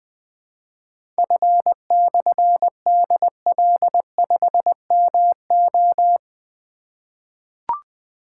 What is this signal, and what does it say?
Morse code, heard as a keyed tone, about 20 wpm, 700 Hz. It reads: F/DL5MO
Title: Difficult Call Signs 20wpm